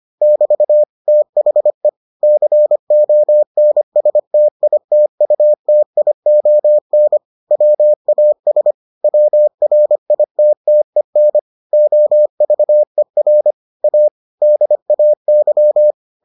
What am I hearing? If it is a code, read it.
Morse code, heard as a keyed tone, about 25 wpm, 600 Hz. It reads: <BT> THE CONSTITUTION WAS WRITTEN OVER A DAY